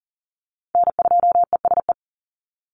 Morse code, heard as a keyed tone, about 40 wpm, 700 Hz. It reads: N2ESE
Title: NA SOTA Chaser Callsigns 40wpm